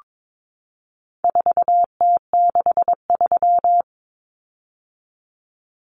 Morse code, heard as a keyed tone, about 22 wpm, 700 Hz. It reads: 4T63
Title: Common Cut Numbers - 4-Digit 22wpm